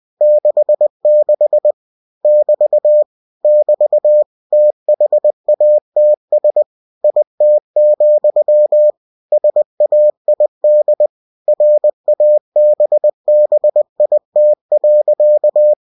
Morse code, heard as a keyed tone, about 20 wpm, 600 Hz. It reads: 66 <BT> <BT> THATS IT, SAID RABBIT.
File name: cw-books / winnie-the-pooh / winnie-20wpm-page-0066